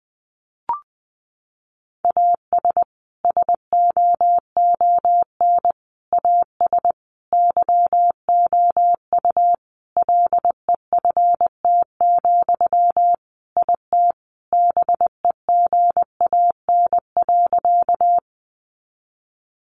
Morse code, heard as a keyed tone, about 20 wpm, 700 Hz. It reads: AS SOON AS YOU LEFT, IT BEGAN.
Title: Sentences from Top 400 Words 20wpm